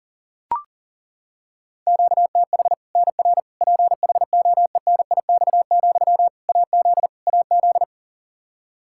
Morse code, heard as a keyed tone, about 40 wpm, 700 Hz. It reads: QTH NR PHOENIX, AZ AZ
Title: Practice QSO - Exchange 1 - Beginning - 40wpm